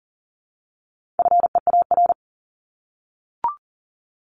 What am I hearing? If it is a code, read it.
Morse code, heard as a keyed tone, about 40 wpm, 700 Hz. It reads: FEAR
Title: Top 1751-2000 Words 40wpm